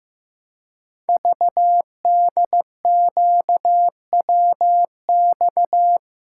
Morse code, heard as a keyed tone, about 15 wpm, 700 Hz. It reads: VDQWX
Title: Code Groups - 5 Letter:Numbers - 15wpm